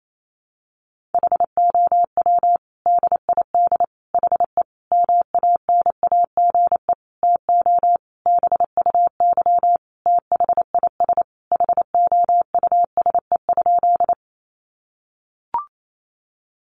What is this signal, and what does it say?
Morse code, heard as a keyed tone, about 28 wpm, 700 Hz. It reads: HOW DID HE MANAGE TO BUY THIS HOUSE?
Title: Sentences from Top 1400 Words 28wpm